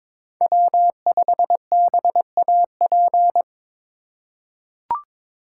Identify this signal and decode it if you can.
Morse code, heard as a keyed tone, about 22 wpm, 700 Hz. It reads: W5BAP
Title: USA Call Signs - 5 Characters - 4x Character Spacing - 22wpm